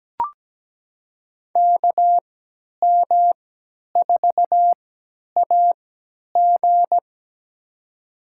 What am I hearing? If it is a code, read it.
Morse code, heard as a keyed tone, about 17 wpm, spaced as for 10 wpm, 700 Hz. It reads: KM4AG
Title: USA Call Signs - 5 Characters - 3x Character Spacing - 17wpm